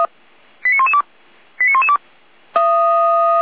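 Five-tone selective call (ZVEI), heard as quick runs of five tones, tones of about 80 ms. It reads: 89112 89112
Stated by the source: ZVEI five-tone selective call received on SDR.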